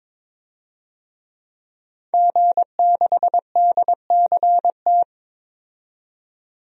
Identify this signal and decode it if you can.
Morse code, heard as a keyed tone, about 22 wpm, 700 Hz. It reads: G6DCT
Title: DXCC-100 Call Signs - 5 Characters - 22wpm